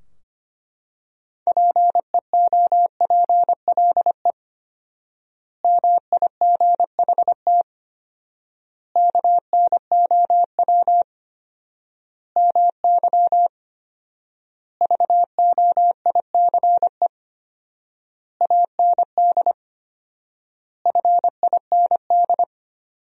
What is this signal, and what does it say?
Morse code, heard as a keyed tone, about 25 wpm, 700 Hz. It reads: PEOPLE MIGHT KNOW MY VOICE AND FIND
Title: 4x Word-Spacing - Sentences from Top 500 Words - 25wpm